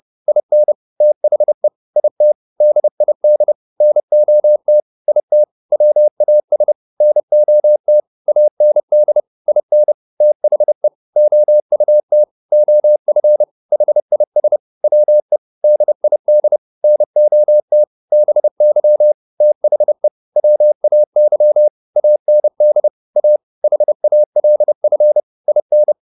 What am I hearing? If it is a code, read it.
Morse code, heard as a keyed tone, about 30 wpm, 600 Hz. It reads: IN THE IT DID NOT IT WAS NOT AND IN THE OUT OF HIS WE DID NOT BY THE WAY AND A HALF IN